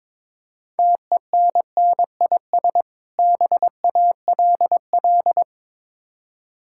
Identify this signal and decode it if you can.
Morse code, heard as a keyed tone, about 22 wpm, 700 Hz. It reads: TENNIS BALL
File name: Difficult Words with Hints - 22wpm